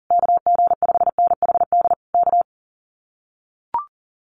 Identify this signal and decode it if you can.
Morse code, heard as a keyed tone, about 40 wpm, 700 Hz. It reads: KG5NHD K
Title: Common QSO Phrases Encore 40wpm